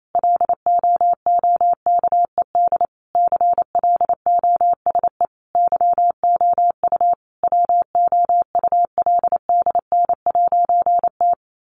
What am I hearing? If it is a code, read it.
Morse code, heard as a keyed tone, about 28 wpm, 700 Hz. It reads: LOOKED CLOSE YOU WOULDN'T
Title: A Sound Of Thunder 28wpm